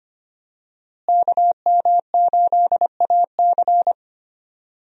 Morse code, heard as a keyed tone, about 25 wpm, 700 Hz. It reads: KM8AC
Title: USA Call Signs - 5 Characters - Repeated 3x - 25wpm